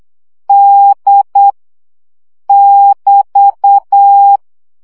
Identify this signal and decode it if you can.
Morse code, heard as a keyed tone, about 8 wpm, 800 Hz. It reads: D <BT>
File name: CW gyakorlas d 7 baud